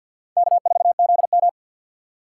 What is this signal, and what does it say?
Morse code, heard as a keyed tone, about 53 wpm, 700 Hz. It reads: K4ZM
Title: USA Call Signs - 4 Characters - 3x Character Spacing - 50wpm